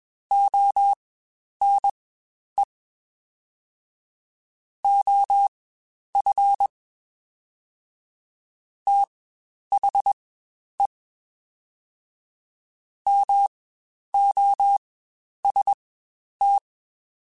Morse code, heard as a keyed tone, about 22 wpm, spaced as for 10 wpm, 800 Hz. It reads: ONE OF THE MOST
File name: Quotes Spaced / Quotes Spaced-mp3s / ChildAsleep sp @20